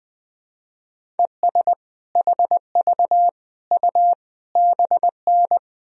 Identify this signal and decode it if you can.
Morse code, heard as a keyed tone, about 21 wpm, 700 Hz. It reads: ES HV U BN
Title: Sentences from Top 200 Words and Abbreviations 20wpm